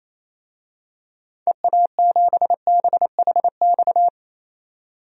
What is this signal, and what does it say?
Morse code, heard as a keyed tone, about 27 wpm, 700 Hz. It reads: EA7BHX
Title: DXCC-100 Call Signs - 6 Characters - 28wpm